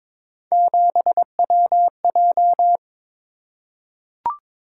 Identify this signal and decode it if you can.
Morse code, heard as a keyed tone, about 22 wpm, 700 Hz. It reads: 7WJ
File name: practice / 22wpm / Three Letter-Number Combinations - 22wpm